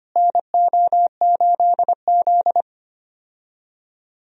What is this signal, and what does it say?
Morse code, heard as a keyed tone, about 25 wpm, 700 Hz. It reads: NO8Z
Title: USA Call Signs - 4 Characters - Repeated 2x - 25wpm